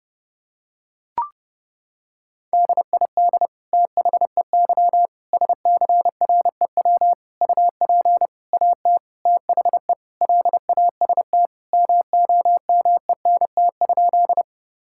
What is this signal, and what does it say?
Morse code, heard as a keyed tone, about 32 wpm, 700 Hz. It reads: DID THEY SCREW UP AT THE LAST MOMENT?
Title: Sentences from Top 1600 Words 30wpm